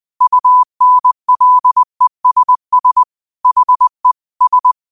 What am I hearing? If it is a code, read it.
Morse code, heard as a keyed tone, about 20 wpm, 1000 Hz. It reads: UNLESS HE S
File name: 20wpm